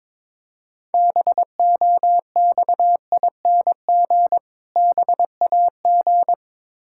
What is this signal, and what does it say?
Morse code, heard as a keyed tone, about 22 wpm, 690 Hz. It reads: BOXING BAG
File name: Difficult Words with Hints - 22wpm